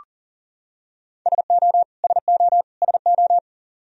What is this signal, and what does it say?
Morse code, heard as a keyed tone, about 40 wpm, 700 Hz. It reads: SO SO SO
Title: Top 100 Words Repeated 40wpm